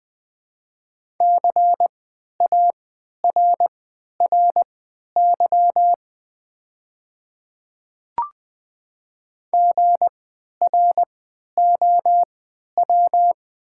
Morse code, heard as a keyed tone, about 20 wpm, spaced as for 11 wpm, 690 Hz. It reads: CARRY GROW
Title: 3x Character Spacing - Top 300 Words - 20wpm